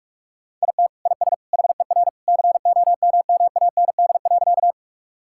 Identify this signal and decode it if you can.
Morse code, heard as a keyed tone, about 46 wpm, 700 Hz. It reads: IT IS HER COMMAND.
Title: Sentences from Top 1600 Words 45wpm